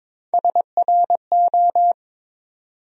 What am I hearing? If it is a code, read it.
Morse code, heard as a keyed tone, about 22 wpm, 700 Hz. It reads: SRO
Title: USA Call Sign Postfixes - 3 Letters 22wpm